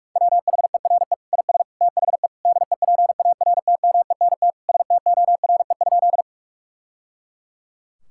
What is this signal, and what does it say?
Morse code, heard as a keyed tone, about 46 wpm, 700 Hz. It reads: WHERE IS THE DEPARTMENT STORE?
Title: Sentences from Top 1000 Words - Review Part 5 - 45wpm